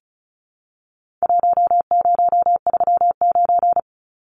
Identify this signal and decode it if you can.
Morse code, heard as a keyed tone, about 34 wpm, 700 Hz. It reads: 1039
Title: Numbers - Contest Style - 4-Digit 35wpm